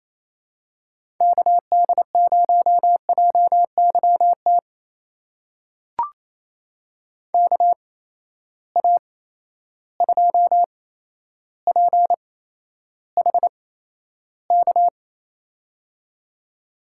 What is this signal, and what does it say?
Morse code, heard as a keyed tone, about 28 wpm, spaced as for 12 wpm, 700 Hz. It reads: KD0JYT KA2PHK
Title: USA Call Signs - 8x Character Spacing - 28wpm